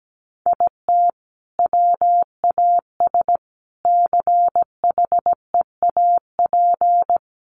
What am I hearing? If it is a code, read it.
Morse code, heard as a keyed tone, about 17 wpm, 700 Hz. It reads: IT WAS CHEAP
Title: Sentences from Top 1200 Words 17wpm